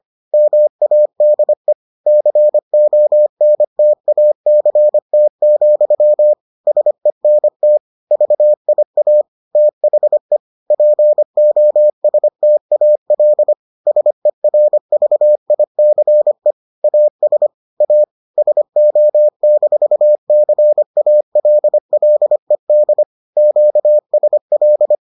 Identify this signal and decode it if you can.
Morse code, heard as a keyed tone, about 25 wpm, 600 Hz. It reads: MADE CONTACT, SENT VIA THE POSTAL SERVICE AS A SO-CALLED QSL